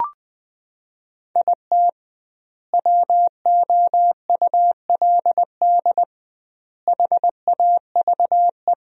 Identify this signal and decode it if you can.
Morse code, heard as a keyed tone, about 20 wpm, 700 Hz. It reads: IT WOULD HAVE
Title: 2x Word-Spacing - Sentences from Top 400 Words - 20wpm